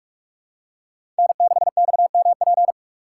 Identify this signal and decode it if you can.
Morse code, heard as a keyed tone, about 43 wpm, 700 Hz. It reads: N6XMP
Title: USA Call Signs - 5 Characters - 2x Character Spacing - 45wpm